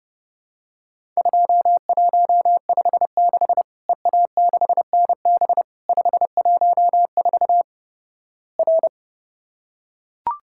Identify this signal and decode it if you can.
Morse code, heard as a keyed tone, about 32 wpm, 700 Hz. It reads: 2156 EA6NB 514 R
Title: Worked All Europe DX Contest - QTC Exchange - 30wpm